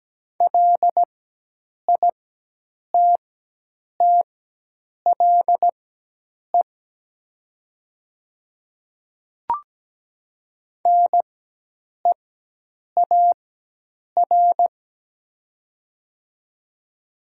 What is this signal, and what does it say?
Morse code, heard as a keyed tone, about 17 wpm, spaced as for 8 wpm, 700 Hz. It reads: LITTLE NEAR
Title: 4x Character Spacing - Top 200 Words - 17wpm